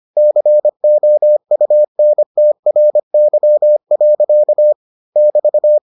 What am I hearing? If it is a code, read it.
Morse code, heard as a keyed tone, about 25 wpm, 600 Hz. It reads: COUNTRY. <BT>